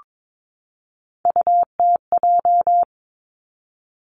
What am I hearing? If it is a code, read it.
Morse code, heard as a keyed tone, about 22 wpm, 700 Hz. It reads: UTJ
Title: Three Challenging Letter-Number Combinations 22wpm